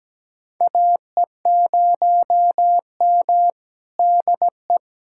Morse code, heard as a keyed tone, about 17 wpm, 700 Hz. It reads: AE0M DE
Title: USA Callsign DE Callsign - 4 Characters - 17wpm